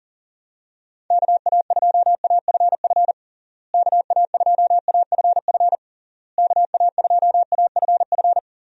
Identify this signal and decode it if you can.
Morse code, heard as a keyed tone, about 40 wpm, 700 Hz. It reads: KA2AFF KA2AFF KA2AFF
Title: USA Call Signs - 6 Characters - Repeated 3x - 40wpm